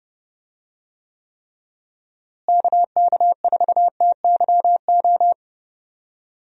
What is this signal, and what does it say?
Morse code, heard as a keyed tone, about 30 wpm, 700 Hz. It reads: KK4TYO
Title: USA Call Signs - 6 Characters - Repeated 3x - 30wpm